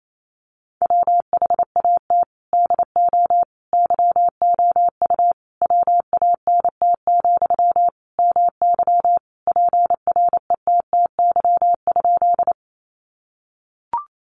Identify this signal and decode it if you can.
Morse code, heard as a keyed tone, about 28 wpm, 700 Hz. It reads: WHAT DO YOU WANT, MY PRETTY?
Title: Sentences from Top 1000 Words 28wpm